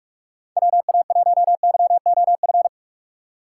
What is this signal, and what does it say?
Morse code, heard as a keyed tone, about 46 wpm, 700 Hz. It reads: WA1YOF
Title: USA Call Signs - 6 Characters - 3x Character Spacing - 45wpm